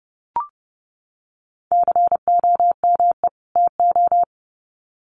Morse code, heard as a keyed tone, about 30 wpm, 700 Hz. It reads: COME TO
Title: Sets of 2 Words 30wpm